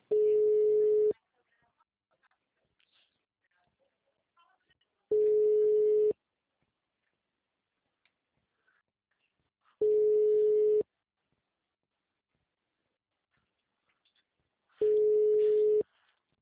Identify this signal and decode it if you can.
Ringback tone (Europe, 425 Hz), with speech.
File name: out- -2038- - - .2793